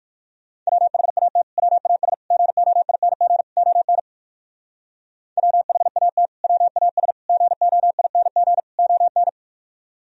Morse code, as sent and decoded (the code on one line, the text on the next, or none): .-- .... .- - / .-- .- ... / --. --- .. -. --. / --- -. / .-- .... .- - / .-- .- ... / --. --- .. -. --. / --- -.
WHAT WAS GOING ON WHAT WAS GOING ON